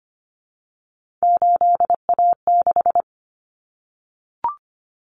Morse code, heard as a keyed tone, about 25 wpm, 700 Hz. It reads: 8A6
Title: Common Cut Numbers - 3-Digit 25wpm